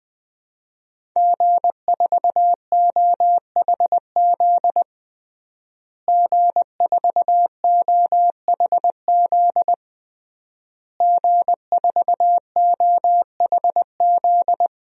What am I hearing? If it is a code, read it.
Morse code, heard as a keyed tone, about 21 wpm, 700 Hz. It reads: G4OHZ G4OHZ G4OHZ
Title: DXCC-100 Call Signs - 5 Characters - Repeated 3x - 20wpm